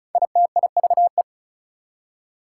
Morse code, heard as a keyed tone, about 36 wpm, 700 Hz. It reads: ITIVE
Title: Prefixes and Suffixes 35wpm